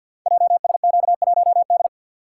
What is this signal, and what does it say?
Morse code, heard as a keyed tone, about 52 wpm, 690 Hz. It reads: JSQ1D
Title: Code Groups - 5 Letter:Numbers - 50wpm